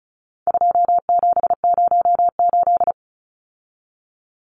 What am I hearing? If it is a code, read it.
Morse code, heard as a keyed tone, about 34 wpm, 700 Hz. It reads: 2708
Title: Numbers - 4-Digit 35wpm